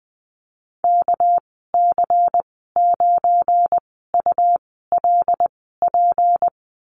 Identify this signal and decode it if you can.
Morse code, heard as a keyed tone, about 20 wpm, spaced as for 14 wpm, 700 Hz. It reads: KC9ULP
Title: 231 Lesson 37g - Letter x - Call Signs 20wpm